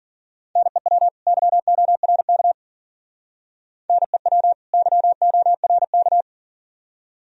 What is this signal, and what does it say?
Morse code, heard as a keyed tone, about 52 wpm, 700 Hz. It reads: NEW YORK NEW YORK